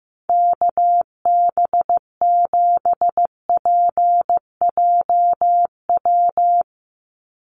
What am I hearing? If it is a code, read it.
Morse code, heard as a keyed tone, about 15 wpm, 700 Hz. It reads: KB7PJW
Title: USA Call Signs 15wpm